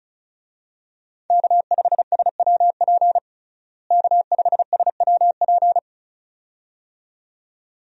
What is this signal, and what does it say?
Morse code, heard as a keyed tone, about 34 wpm, 700 Hz. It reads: K5SWP K5SWP
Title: USA Call Signs - 5 Characters - Repeated 2x - 35wpm